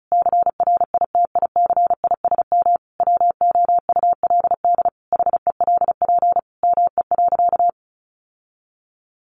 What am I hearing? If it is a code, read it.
Morse code, heard as a keyed tone, about 34 wpm, 700 Hz. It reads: CRITICISM WOULD HELP ME.
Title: Sentences from Top 1300 Words 35wpm